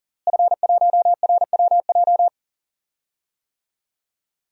Morse code, heard as a keyed tone, about 40 wpm, 700 Hz. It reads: F1RWJ
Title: DXCC-100 Call Signs - 5 Characters - Repeated 3x - 40wpm